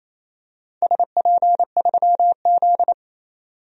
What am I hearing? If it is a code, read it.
Morse code, heard as a keyed tone, about 28 wpm, 700 Hz. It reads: SP3Z
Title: DXCC-100 Call Signs - 4 Characters - 28wpm